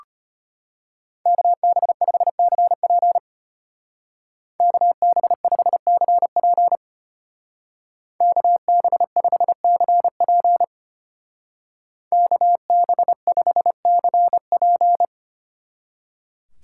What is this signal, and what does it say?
Morse code, heard as a keyed tone, about 30 wpm, 700 Hz. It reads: KB5CP KB5CP KB5CP KB5CP
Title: Speed Racing - USA 2x2 Call Signs 25wpm